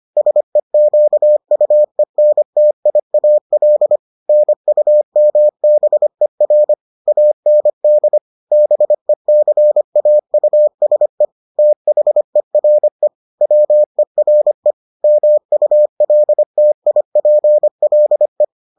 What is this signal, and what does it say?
Morse code, heard as a keyed tone, about 25 wpm, 600 Hz. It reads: SEQUENTIAL NUMBER AND BECAUSE THERE WERE MULTIPLE